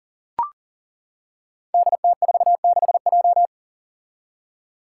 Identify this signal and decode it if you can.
Morse code, heard as a keyed tone, about 41 wpm, 700 Hz. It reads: DT46J
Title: Five Challenging Letter-Number Combinations 40wpm